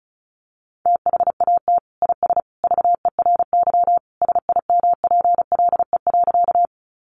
Morse code, heard as a keyed tone, about 34 wpm, 700 Hz. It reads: THAT IS VERY SIMPLE.
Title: Sentences from Top 400 Words 35wpm